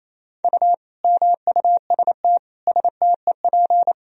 Morse code, heard as a keyed tone, about 28 wpm, 700 Hz. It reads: U MUST STEP
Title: Sentences from Top 400 Words and Abbreviations 28wpm